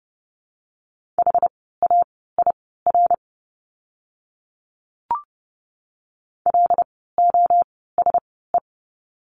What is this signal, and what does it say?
Morse code, heard as a keyed tone, about 29 wpm, spaced as for 17 wpm, 700 Hz. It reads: HAIR LOSE
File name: Lesson 17f - Letter u - Words 30wpm